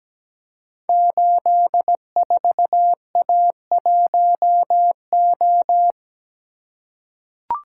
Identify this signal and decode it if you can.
Morse code, heard as a keyed tone, about 17 wpm, 700 Hz. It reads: 84A1O
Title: Code Groups - 5 Letter:Numbers - 17wpm